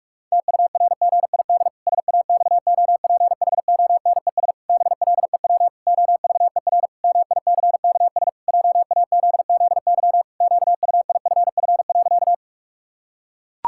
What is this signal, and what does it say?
Morse code, heard as a keyed tone, about 46 wpm, 700 Hz. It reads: TURGID SAXOPHONES BLEW OVER MICKS JAZZY QUIFF.
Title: Pangram Challenge v2 - 45wpm